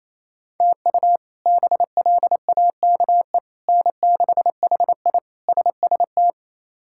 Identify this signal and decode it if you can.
Morse code, heard as a keyed tone, about 28 wpm, 700 Hz. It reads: TU BLAKE N6HI SST
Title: K1USN SST QSO Practice - 28wpm